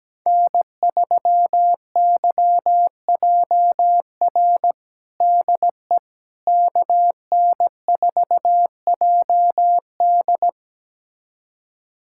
Morse code, heard as a keyed tone, about 17 wpm, 700 Hz. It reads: N3YJR DE KN4JD
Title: USA Callsign DE Callsign - 5 Characters - 17wpm